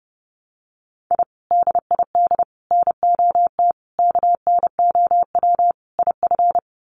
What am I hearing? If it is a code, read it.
Morse code, heard as a keyed tone, about 30 wpm, 700 Hz. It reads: I DID NOT KNOW IF
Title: Sets of 5 Words Encore 30wpm